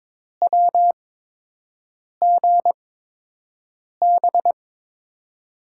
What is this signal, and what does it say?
Morse code, heard as a keyed tone, about 22 wpm, spaced as for 6 wpm, 700 Hz. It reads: WGB
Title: USA Call Signs - 8x Character Spacing - 22wpm